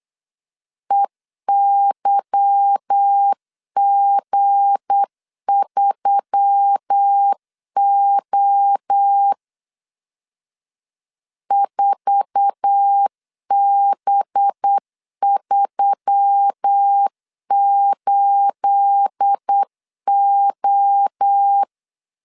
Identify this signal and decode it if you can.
Morse code, heard as a keyed tone, about 8 wpm, 800 Hz. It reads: EYG3O 4B38O